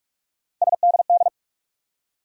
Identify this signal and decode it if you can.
Morse code, heard as a keyed tone, about 42 wpm, 700 Hz. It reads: SDD
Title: USA Call Sign Postfixes - 3 Letters 45wpm